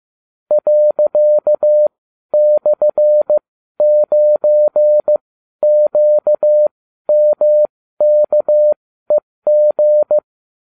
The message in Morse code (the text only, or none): ./9QMKEG